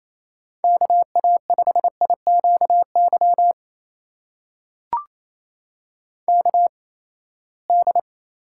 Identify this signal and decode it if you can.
Morse code, heard as a keyed tone, about 28 wpm, spaced as for 19 wpm, 700 Hz. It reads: KA5IQY KD
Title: USA Call Signs - 6 Characters - 8x Character Spacing - 28wpm